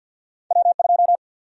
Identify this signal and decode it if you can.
Morse code, heard as a keyed tone, about 46 wpm, 700 Hz. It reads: W2
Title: USA Call Sign Prefixes - 50wpm